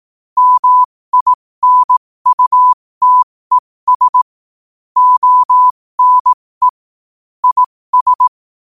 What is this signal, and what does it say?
Morse code, heard as a keyed tone, about 18 wpm, spaced as for 15 wpm, 1000 Hz. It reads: MINUTES ONE IS
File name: qotdfiles / 15wpm